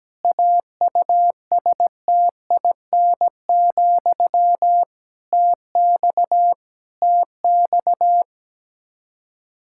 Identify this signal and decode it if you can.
Morse code, heard as a keyed tone, about 17 wpm, 700 Hz. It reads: AUSTIN, TX TX
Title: Practice QSO - Exchange 1 - Beginning - 17wpm